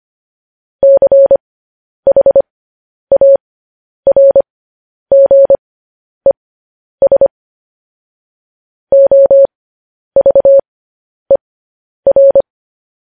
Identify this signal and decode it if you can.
Morse code, heard as a keyed tone, about 25 wpm, spaced as for 10 wpm, 550 Hz. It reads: CHARGES OVER
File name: podcast / News headlines-1x-25x10